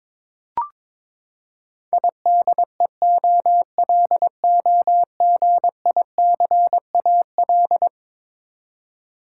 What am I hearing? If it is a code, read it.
Morse code, heard as a keyed tone, about 22 wpm, 700 Hz. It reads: IDEOLOGICAL
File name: practice / 22wpm / Eleven Letter Words - 22wpm